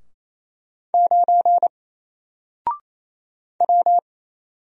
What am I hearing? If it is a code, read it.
Morse code, heard as a keyed tone, about 27 wpm, 700 Hz. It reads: 9 W
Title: Single Letter-Number Alt 28wpm